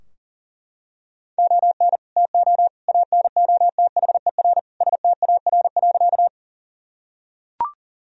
Morse code, heard as a keyed tone, about 40 wpm, 700 Hz. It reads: ON TO ANOTHER STAR.
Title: Sentences from Top 1000 Words - Review Part 2 - 40wpm